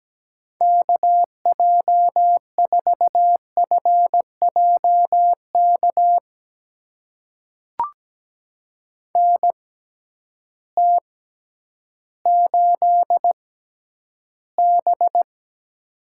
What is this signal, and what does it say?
Morse code, heard as a keyed tone, about 17 wpm, spaced as for 10 wpm, 700 Hz. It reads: KJ4FJK NT8B
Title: USA Call Signs - 6x Character Spacing - 17wpm